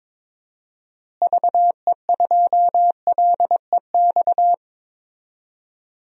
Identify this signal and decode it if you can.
Morse code, heard as a keyed tone, about 22 wpm, 700 Hz. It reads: VE2LEX
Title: DXCC-100 Call Signs - 6 Characters - Repeated 2x - 22wpm